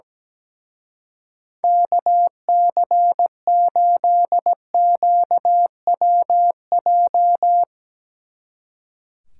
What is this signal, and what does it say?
Morse code, heard as a keyed tone, about 18 wpm, 700 Hz. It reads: KC8QWJ
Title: Speed Racing - USA 2x3 Call Signs 17wpm